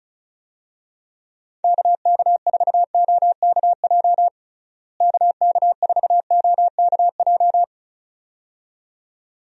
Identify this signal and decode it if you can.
Morse code, heard as a keyed tone, about 34 wpm, 700 Hz. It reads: KK4OKJ KK4OKJ
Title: USA Call Signs - 6 Characters - Repeated 2x - 35wpm